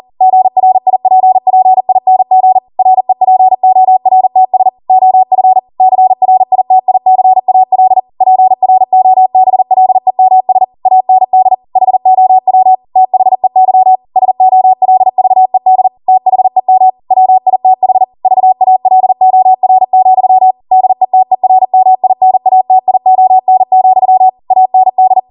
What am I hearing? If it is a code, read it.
Morse code, heard as a keyed tone, about 40 wpm, 750 Hz. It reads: GRIPPING REPORTS OF CRITICAL PROBLEMS AND HOW THEY SOLVED THEM WITH VALOR, DETERMINATION, AND